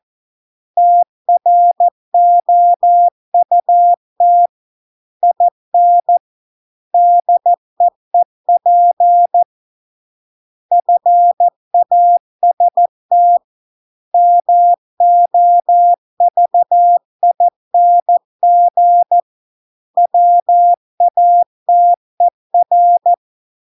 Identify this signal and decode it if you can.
Morse code, heard as a keyed tone, about 14 wpm, 700 Hz. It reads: TROUT IN DEEP FAST MOVING WATER